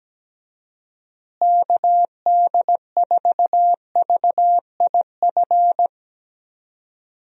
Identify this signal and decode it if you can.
Morse code, heard as a keyed tone, about 17 wpm, 700 Hz. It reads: KD4VIF
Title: USA Call Signs - 6 Characters - Repeated 3x - 17wpm